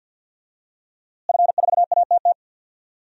Morse code, heard as a keyed tone, about 50 wpm, 700 Hz. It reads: F4ATT
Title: DXCC-100 Call Signs - 5 Characters - Repeated 3x - 50wpm